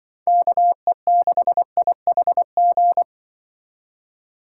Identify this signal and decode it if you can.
Morse code, heard as a keyed tone, about 24 wpm, 700 Hz. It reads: KE6IHG
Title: Speed Racing - USA Call Signs 20wpm